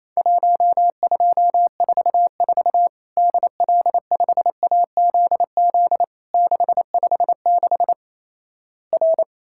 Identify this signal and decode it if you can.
Morse code, heard as a keyed tone, about 28 wpm, 700 Hz. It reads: 1244 DL5AZZ 656 R
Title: Worked All Europe DX Contest - QTC Exchange - 28wpm